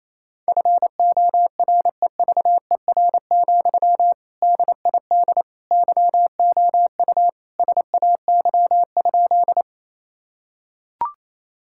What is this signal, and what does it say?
Morse code, heard as a keyed tone, about 28 wpm, 700 Hz. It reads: FOREVER, DID YOU SAY?
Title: Sentences from Top 1600 Words 28wpm